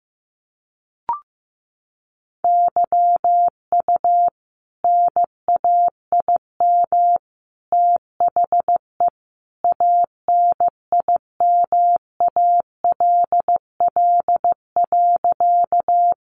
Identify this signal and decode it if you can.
Morse code, heard as a keyed tone, about 15 wpm, 700 Hz. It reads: YU NAIM THE ANIMALL.
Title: A Sound Of Thunder 15wpm